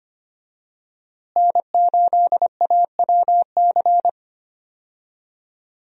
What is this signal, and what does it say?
Morse code, heard as a keyed tone, about 25 wpm, 700 Hz. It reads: N8AWC
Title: USA Call Signs - 5 Characters - Repeated 2x - 25wpm